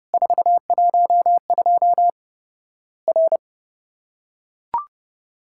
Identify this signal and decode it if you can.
Morse code, heard as a keyed tone, about 30 wpm, 700 Hz. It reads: 412 R
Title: Worked All Europe DX Contest - QTC Exchange - 30wpm